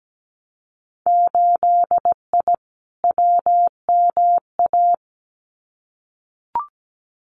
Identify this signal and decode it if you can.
Morse code, heard as a keyed tone, about 17 wpm, 700 Hz. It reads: 8I WMA
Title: Winter Field Day Exchanges Encore 17wpm